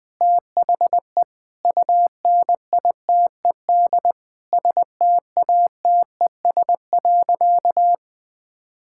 Morse code, heard as a keyed tone, about 21 wpm, 700 Hz. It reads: THE UNITED STATES.
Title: Sentences with 4-Word Phrases - 20wpm